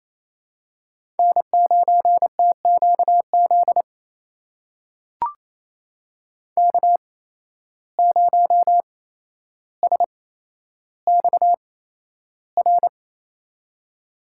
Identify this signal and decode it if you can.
Morse code, heard as a keyed tone, about 28 wpm, spaced as for 12 wpm, 700 Hz. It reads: N9TQZ K0SXR
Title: USA Call Signs - 5 Characters - 8x Character Spacing - 28wpm